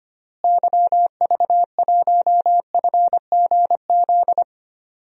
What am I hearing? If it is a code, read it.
Morse code, heard as a keyed tone, about 25 wpm, 700 Hz. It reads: YV1FGZ
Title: DXCC-100 Call Signs - 6 Characters - Repeated 3x - 25wpm